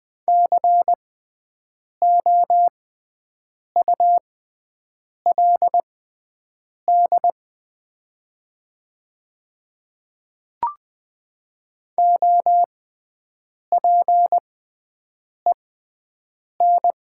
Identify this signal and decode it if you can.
Morse code, heard as a keyed tone, about 20 wpm, spaced as for 7 wpm, 700 Hz. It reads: COULD OPEN
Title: 6x Character Spacing - Top 500 Words - 20wpm